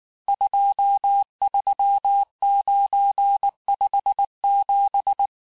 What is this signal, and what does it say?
Morse code, heard as a keyed tone, about 19 wpm, 800 Hz. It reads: 23957
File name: cours-du-lundi files / cours-1589-texte-chiffre-vitesse-18mpm